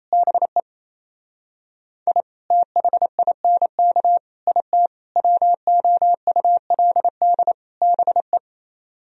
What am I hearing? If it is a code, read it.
Morse code, heard as a keyed tone, about 36 wpm, 700 Hz. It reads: BE I THINK IT WOULD BE